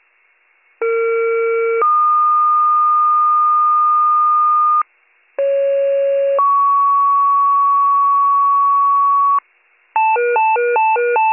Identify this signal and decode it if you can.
Two-tone page, heard as a one-second tone, then a three-second tone, about 460 then 1220 Hz, about 550 then 1120 Hz. It is heard from 0.8 to 9.4 s.